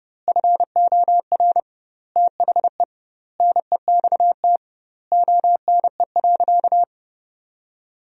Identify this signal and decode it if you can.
Morse code, heard as a keyed tone, about 32 wpm, 700 Hz. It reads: FOR THE NEXT ONE.
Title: 2x Word-Spacing - Sentences from Top 400 Words - 30wpm